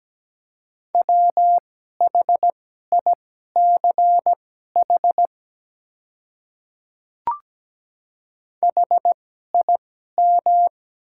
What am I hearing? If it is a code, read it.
Morse code, heard as a keyed tone, about 17 wpm, spaced as for 12 wpm, 700 Hz. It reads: WHICH HIM
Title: 2x Character Spacing - Top 100 Words - 17wpm